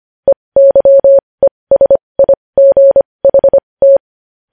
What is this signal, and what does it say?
Morse code, heard as a keyed tone, about 25 wpm, spaced as for 20 wpm, 550 Hz. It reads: EYESIGHT
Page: News Headlines in Morse Code at 20 WPM